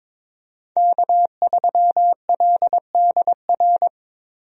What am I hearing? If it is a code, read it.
Morse code, heard as a keyed tone, about 22 wpm, 700 Hz. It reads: K3LDR
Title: USA Call Signs - 5 Characters - Repeated 3x - 22wpm